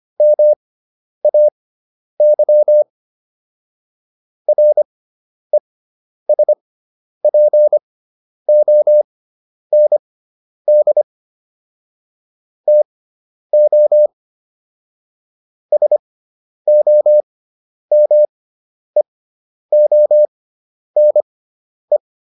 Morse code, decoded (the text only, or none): MAY RESPOND TO SOMEONE